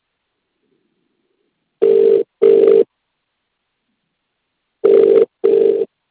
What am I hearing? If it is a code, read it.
Ringback tone (UK), 400 + 450 Hz.